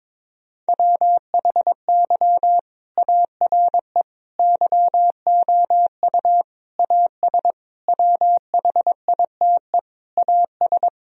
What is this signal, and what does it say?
Morse code, heard as a keyed tone, about 22 wpm, 700 Hz. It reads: WHY ARE YOU AS WHITE AS
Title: Sentences from Top 1000 Words - Review Part 3 - 22wpm